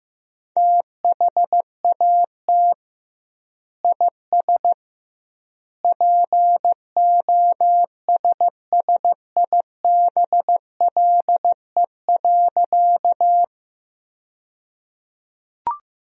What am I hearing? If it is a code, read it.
Morse code, heard as a keyed tone, about 15 wpm, 700 Hz. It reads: THAT IS POSSIBLE.
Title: 2x Word-Spacing - Sentences from Top 500 Words - 15wpm